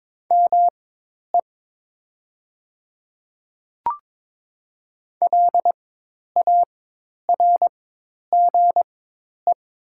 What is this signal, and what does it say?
Morse code, heard as a keyed tone, about 22 wpm, spaced as for 10 wpm, 700 Hz. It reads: ME LARGE
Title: 4x Character Spacing - Top 200 Words - 22wpm